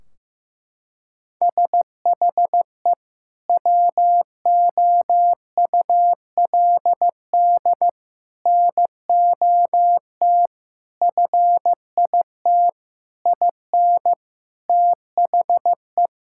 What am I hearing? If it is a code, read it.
Morse code, heard as a keyed tone, about 15 wpm, 700 Hz. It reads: SHE WOULD NOT FIT IN THE
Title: Sentences from Top 1600 Words 15wpm